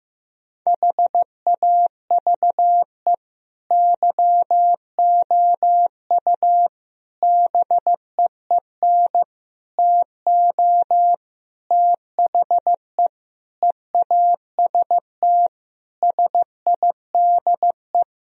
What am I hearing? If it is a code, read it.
Morse code, heard as a keyed tone, about 16 wpm, 700 Hz. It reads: HAVE YOU BEEN TO THE EAST SIDE
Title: Sentences from Top 1000 Words - Review Part 3 - 15wpm